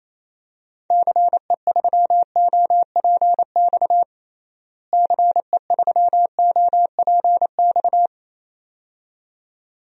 Morse code, heard as a keyed tone, about 28 wpm, 700 Hz. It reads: CE3OPX CE3OPX
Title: DXCC-100 Call Signs - 6 Characters - Repeated 2x - 28wpm